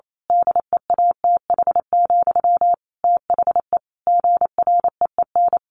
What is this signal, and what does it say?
Morse code, heard as a keyed tone, about 28 wpm, 700 Hz. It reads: DEATH, THE GREEN
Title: A Sound Of Thunder 28wpm